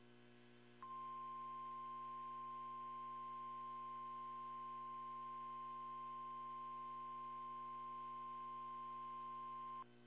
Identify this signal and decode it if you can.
Emergency Alert System (SAME) tones (the 1050 Hz attention tone), with noise. It is heard from 0.8 to 9.9 s.
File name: eas / KSDZKDJL / KSDZKDJL